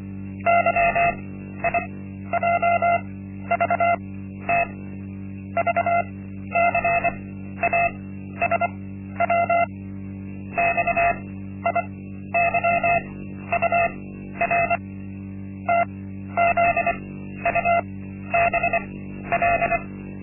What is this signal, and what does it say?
Morse code, heard as a keyed tone, about 24 wpm, spaced as for 12 wpm, 690 Hz. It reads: YIJVT VCASW XIYUR TZUBL
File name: Linksmakalnis-1